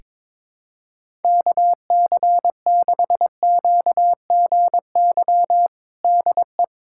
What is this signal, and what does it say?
Morse code, heard as a keyed tone, about 21 wpm, 700 Hz. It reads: KC6QGY DE
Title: USA Callsign DE Callsign - 6 Characters - 22wpm